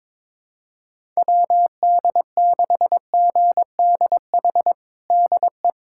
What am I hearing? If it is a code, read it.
Morse code, heard as a keyed tone, about 22 wpm, 700 Hz. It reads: WD6GDH DE
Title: USA Callsign DE Callsign - 6 Characters - 22wpm